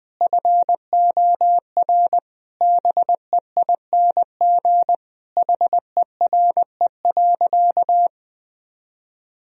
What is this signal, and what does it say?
Morse code, heard as a keyed tone, about 20 wpm, 700 Hz. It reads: FOR BEING HERE.
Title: Sentences from Top 1000 Words - Review Part 3 - 20wpm